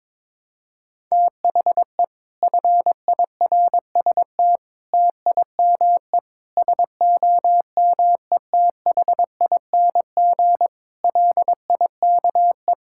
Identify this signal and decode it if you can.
Morse code, heard as a keyed tone, about 22 wpm, 700 Hz. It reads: THE FIRST TIME SOMETHING LIKE
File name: practice / 22wpm / Sets of 8 Words - 22wpm